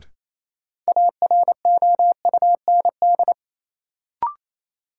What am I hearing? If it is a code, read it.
Morse code, heard as a keyed tone, about 28 wpm, 700 Hz. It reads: AROUND
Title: Sentences from Top 600 Words - New Word Drill - 28wpm